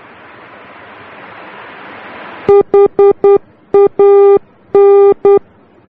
Morse code, heard as a keyed tone, about 10 wpm, 400 Hz. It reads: HAN
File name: germany / mx / HAN-D